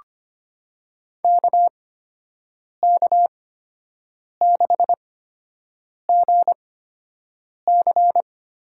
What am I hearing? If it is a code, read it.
Morse code, heard as a keyed tone, about 25 wpm, spaced as for 7 wpm, 700 Hz. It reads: KK6GC
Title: USA Call Signs - 8x Character Spacing - 25wpm